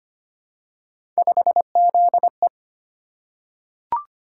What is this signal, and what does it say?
Morse code, heard as a keyed tone, about 25 wpm, 700 Hz. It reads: 5ZE
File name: Three Letter-Number Combinations - 25wpm